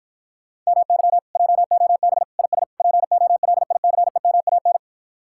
Morse code, heard as a keyed tone, about 57 wpm, 700 Hz. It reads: MY JOB IS POLICEMAN